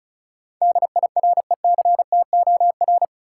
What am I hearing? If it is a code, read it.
Morse code, heard as a keyed tone, about 34 wpm, 700 Hz. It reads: DIRECTOR
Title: Sentences from Top 1300 Words - New Word Drill Alt - 35wpm